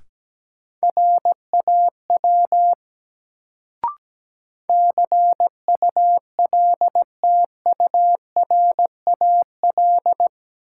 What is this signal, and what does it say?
Morse code, heard as a keyed tone, about 17 wpm, 700 Hz. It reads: RAW CULTURAL
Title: Sentences from Top 1500 Words - New Word Drill - 17wpm